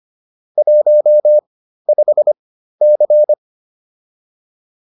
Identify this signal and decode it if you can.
Morse code, heard as a keyed tone, about 25 wpm, spaced as for 13 wpm, 600 Hz. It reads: 15C